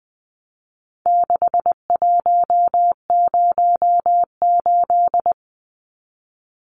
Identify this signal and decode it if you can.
Morse code, heard as a keyed tone, about 20 wpm, 700 Hz. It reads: 6108
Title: Numbers - 4-Digit 20wpm